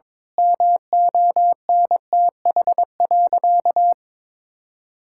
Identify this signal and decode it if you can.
Morse code, heard as a keyed tone, about 22 wpm, 700 Hz. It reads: MONTH.
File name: Difficult Sentences with Hints - 22wpm